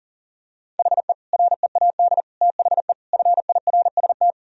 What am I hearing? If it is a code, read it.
Morse code, heard as a keyed tone, about 37 wpm, 700 Hz. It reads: HE READ THE FIRST
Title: Sentences from Top 1000 Words - Review Part 1 - 40wpm